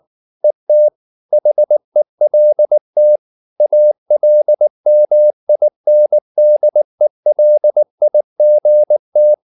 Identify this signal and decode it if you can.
Morse code, heard as a keyed tone, about 19 wpm, 600 Hz. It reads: ET HELT ALMINDELIGT
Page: Ferie 19 wpm